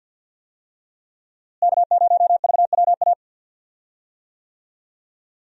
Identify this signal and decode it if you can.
Morse code, heard as a keyed tone, about 46 wpm, 700 Hz. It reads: K0VWA
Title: USA Call Signs - 5 Characters - Repeated 3x - 50wpm